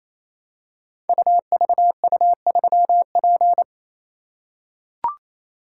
Morse code, heard as a keyed tone, about 27 wpm, 700 Hz. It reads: UVU3P
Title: Code Groups - 5 Letter:Numbers - 28wpm